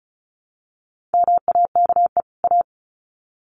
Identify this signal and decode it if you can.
Morse code, heard as a keyed tone, about 34 wpm, 700 Hz. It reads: MAKE A
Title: Sets of 2 Words 35wpm